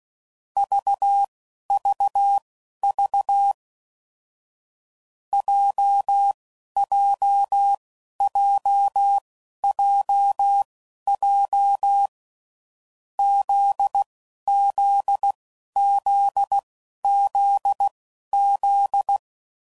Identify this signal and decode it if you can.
Morse code, heard as a keyed tone, about 16 wpm, spaced as for 11 wpm, 800 Hz. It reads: VVV JJJJJ ZZZZZ
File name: Groups / Basic Letter Groups / JZ17